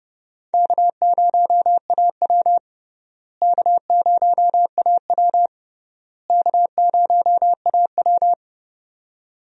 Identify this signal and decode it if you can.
Morse code, heard as a keyed tone, about 30 wpm, 700 Hz. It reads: K0AW K0AW K0AW
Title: USA Call Signs - 4 Characters - Repeated 3x - 30wpm